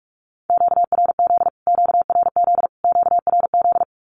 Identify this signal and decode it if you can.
Morse code, heard as a keyed tone, about 45 wpm, 700 Hz. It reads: QRZ QRZ QRZ
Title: Q-Codes Repeated 3x 45wpm